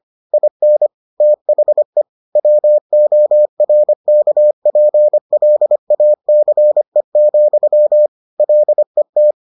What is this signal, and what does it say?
Morse code, heard as a keyed tone, about 25 wpm, 600 Hz. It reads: IN THE WORKPLACE, LET